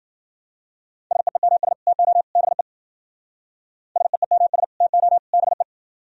Morse code, heard as a keyed tone, about 53 wpm, 700 Hz. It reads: SEEMS TO BE SEEMS TO BE